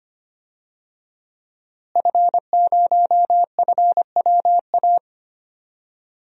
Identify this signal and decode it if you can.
Morse code, heard as a keyed tone, about 25 wpm, 700 Hz. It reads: F0FWA
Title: DXCC-100 Call Signs - 5 Characters - Repeated 2x - 25wpm